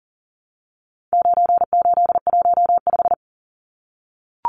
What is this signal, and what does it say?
Morse code, heard as a keyed tone, about 40 wpm, 700 Hz. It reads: 9815
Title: Numbers - 4-Digit 40wpm